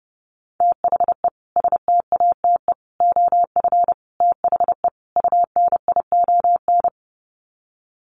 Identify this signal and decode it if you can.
Morse code, heard as a keyed tone, about 30 wpm, 700 Hz. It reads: THE STATE OF THE UNION
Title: Sets of 5 Words Encore 30wpm